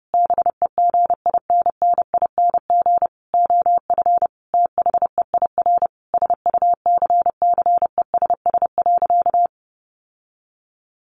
Morse code, heard as a keyed tone, about 30 wpm, 700 Hz. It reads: BEGINNING OF THEIR SUCCESS.
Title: Sentences from Top 900 Words 30wpm